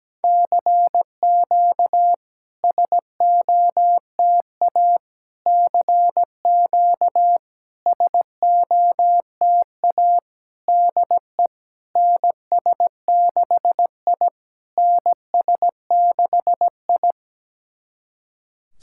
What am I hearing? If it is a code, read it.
Morse code, heard as a keyed tone, about 17 wpm, 700 Hz. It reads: CQ SOTA CQ SOTA DE NS6I NS6I
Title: SOTA QSO Practice 17wpm